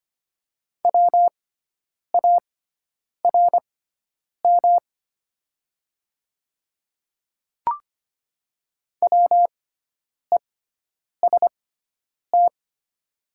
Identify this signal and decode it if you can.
Morse code, heard as a keyed tone, about 25 wpm, spaced as for 9 wpm, 700 Hz. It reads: WARM WEST
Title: 6x Character Spacing - Top 500 Words - 25wpm